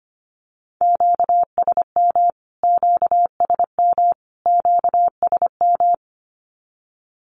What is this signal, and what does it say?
Morse code, heard as a keyed tone, about 25 wpm, 700 Hz. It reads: QSM QSM QSM
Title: Q-Codes Repeated 3x 25wpm